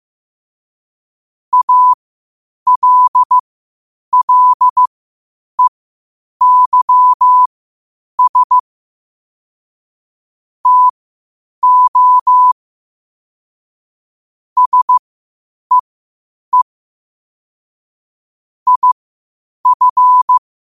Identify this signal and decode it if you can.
Morse code, heard as a keyed tone, about 15 wpm, spaced as for 8 wpm, 1000 Hz. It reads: ALLEYS TO SEE IF
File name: qotdfiles / 8wpm